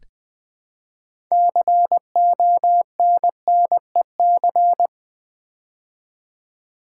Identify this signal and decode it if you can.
Morse code, heard as a keyed tone, about 20 wpm, 700 Hz. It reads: CONNEC
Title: Build A Word - 3 Letters At A Time - 9-Letter Words - 20wpm